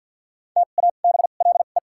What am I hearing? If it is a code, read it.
Morse code, heard as a keyed tone, about 53 wpm, 700 Hz. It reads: T A B L E
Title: 2x Character Spacing - Top 500 Words - 50wpm